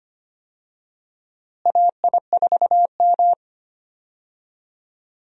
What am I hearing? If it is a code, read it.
Morse code, heard as a keyed tone, about 25 wpm, 700 Hz. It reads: AI4M
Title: USA Call Signs - 4 Characters - Repeated 3x - 25wpm